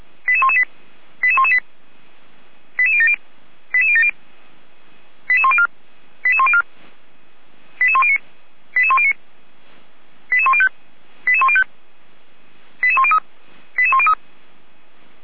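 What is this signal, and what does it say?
Five-tone selective call (ZVEI), heard as quick runs of five tones, tones of about 70 ms. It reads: 88118 88118 88070 88070 88104 88104 88109 88109 88105 88105 88103 88103
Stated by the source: Five-tone ZVEI selective call received on SDR.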